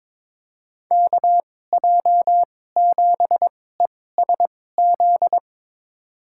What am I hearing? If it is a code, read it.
Morse code, heard as a keyed tone, about 21 wpm, spaced as for 16 wpm, 700 Hz. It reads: KJ7ESZ
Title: USA Call Signs - 2x Character Spacing - 22wpm